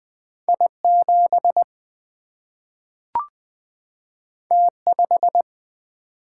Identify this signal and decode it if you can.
Morse code, heard as a keyed tone, about 21 wpm, 700 Hz. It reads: I7 T5
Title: Single-Letter Single-Number 20wpm